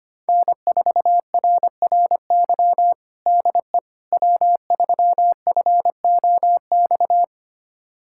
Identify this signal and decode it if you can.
Morse code, heard as a keyed tone, about 25 wpm, 700 Hz. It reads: N4RRY DE W3FOX
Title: USA Callsign DE Callsign - 5 Characters - 25wpm